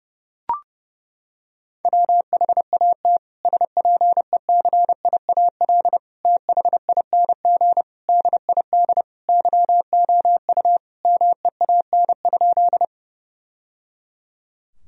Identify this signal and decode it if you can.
Morse code, heard as a keyed tone, about 32 wpm, 700 Hz. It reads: WHAT SPECIAL THING DID YOU MEAN?
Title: Sentences from Top 1000 Words - Review Part 3 - 30wpm